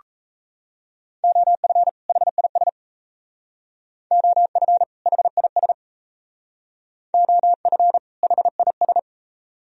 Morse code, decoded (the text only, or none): OF HIS OF HIS OF HIS